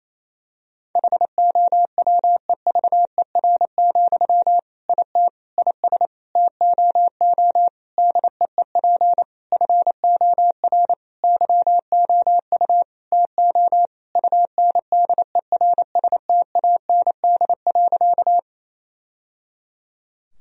Morse code, heard as a keyed tone, about 28 wpm, 700 Hz. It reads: HOWEVER, IT IS TOO DEEP FOR YOU TO UNDERSTAND.
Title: Sentences from Top 1000 Words - Review Part 3 - 28wpm